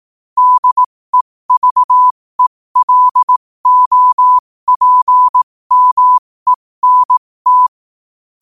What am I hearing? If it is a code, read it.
Morse code, heard as a keyed tone, about 18 wpm, spaced as for 15 wpm, 1000 Hz. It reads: DEVELOPMENT
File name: qotdfiles / 15wpm